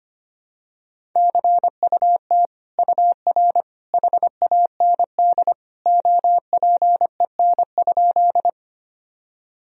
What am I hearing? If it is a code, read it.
Morse code, heard as a keyed tone, about 25 wpm, 700 Hz. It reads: CUT UR HAND OPEN?
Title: Sentences from Top 300 Words and Abbreviations 25wpm